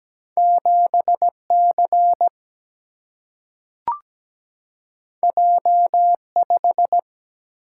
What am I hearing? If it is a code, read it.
Morse code, heard as a keyed tone, about 17 wpm, 700 Hz. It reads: 7C J5
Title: Single-Letter Single-Number 17wpm